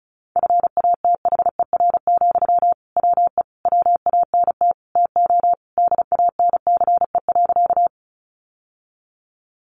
Morse code, heard as a keyed tone, about 34 wpm, 700 Hz. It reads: FATHER, WE WANT TO DANCE.
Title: Sentences from Top 700 Words 35wpm